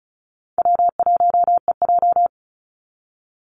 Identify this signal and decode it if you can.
Morse code, heard as a keyed tone, about 34 wpm, 700 Hz. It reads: W1EJ
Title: NA SOTA Activator Callsigns 35wpm